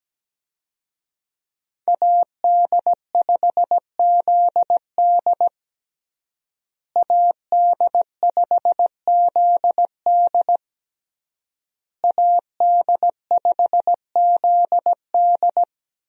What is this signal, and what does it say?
Morse code, heard as a keyed tone, about 17 wpm, 700 Hz. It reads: AD5ZD AD5ZD AD5ZD
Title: USA Call Signs - 5 Characters - Repeated 3x - 17wpm